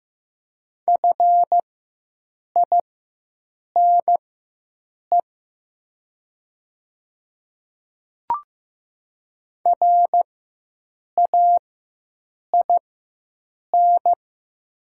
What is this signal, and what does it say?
Morse code, heard as a keyed tone, about 15 wpm, spaced as for 7 wpm, 700 Hz. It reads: FINE RAIN
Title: 4x Character Spacing - Top 500 Words - 15wpm